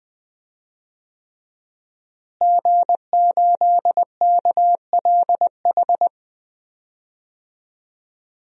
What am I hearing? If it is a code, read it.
Morse code, heard as a keyed tone, about 20 wpm, 700 Hz. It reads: G8KLH
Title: DXCC-100 Call Signs - 5 Characters - Repeated 3x - 20wpm